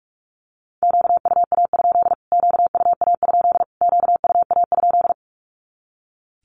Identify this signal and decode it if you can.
Morse code, heard as a keyed tone, about 44 wpm, 700 Hz. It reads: QUA? QUA? QUA?
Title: Q-Codes Repeated 3x 45wpm